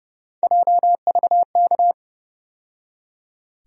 Morse code, heard as a keyed tone, about 31 wpm, 700 Hz. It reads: JVK
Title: USA Call Sign Postfixes - 3 Letters 30wpm